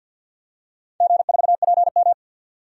Morse code, heard as a keyed tone, about 46 wpm, 700 Hz. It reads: G4PM
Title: DXCC-100 Call Signs - 4 Characters - 50wpm